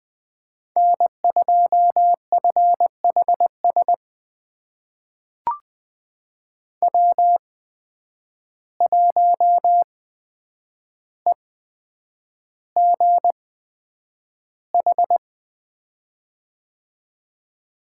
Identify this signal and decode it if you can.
Morse code, heard as a keyed tone, about 20 wpm, spaced as for 9 wpm, 700 Hz. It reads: N2FHS W1EGH
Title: USA Call Signs - 5 Characters - 8x Character Spacing - 20wpm